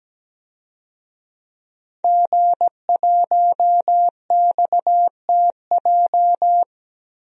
Morse code, heard as a keyed tone, about 17 wpm, 700 Hz. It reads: G1XTJ
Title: DXCC-100 Call Signs - 5 Characters - Repeated 3x - 17wpm